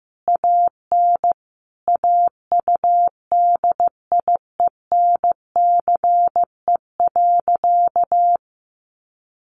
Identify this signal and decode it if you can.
Morse code, heard as a keyed tone, about 15 wpm, 700 Hz. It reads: AN AUDIENCE.
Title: Sentences from Top 800 Words 15wpm